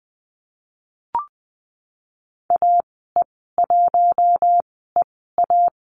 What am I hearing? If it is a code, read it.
Morse code, heard as a keyed tone, about 20 wpm, spaced as for 15 wpm, 700 Hz. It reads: AE1EA
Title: 042 Lesson 8g - Number 1 - Call Signs 20wpm